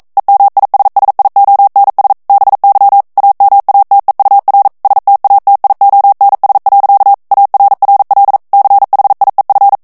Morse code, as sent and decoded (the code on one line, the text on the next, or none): . -- .. ... ... .. --- -. ... / -... -.-- / .- -- .- - . ..- .-. / ... - .- - .. --- -. ... .-.-.- / .- .-. .-. .-.. / -.-. .... .. ..-.
EMISSIONS BY AMATEUR STATIONS. ARRL CHIEF